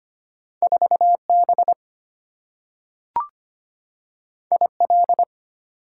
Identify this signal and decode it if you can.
Morse code, heard as a keyed tone, about 25 wpm, 700 Hz. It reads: 4B IL
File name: Two Letter-Number Combinations - 25wpm